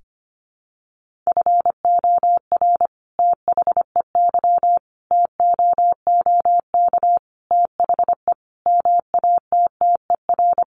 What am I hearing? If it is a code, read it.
Morse code, heard as a keyed tone, about 25 wpm, 700 Hz. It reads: FOR THEY TOOK THE MATTER
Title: Sentences from Top 1400 Words 25wpm